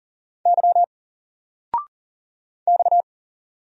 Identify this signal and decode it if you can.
Morse code, heard as a keyed tone, about 40 wpm, 700 Hz. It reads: Y X
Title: Single Letter-Number 40wpm